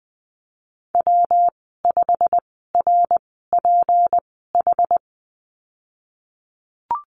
Morse code, heard as a keyed tone, about 20 wpm, spaced as for 15 wpm, 700 Hz. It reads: W5RPH
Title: 167 Lesson 27g - Number 7 - Call Signs 20wpm